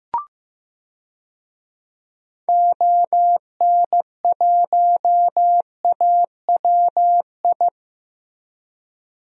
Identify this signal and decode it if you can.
Morse code, heard as a keyed tone, about 16 wpm, 700 Hz. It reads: ON1AWI
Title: DXCC-100 Call Signs - 6 Characters - 15wpm